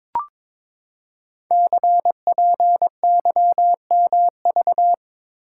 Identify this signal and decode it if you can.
Morse code, heard as a keyed tone, about 23 wpm, 700 Hz. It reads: CPYMV
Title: Code Groups - 5 Letter:Numbers - 22wpm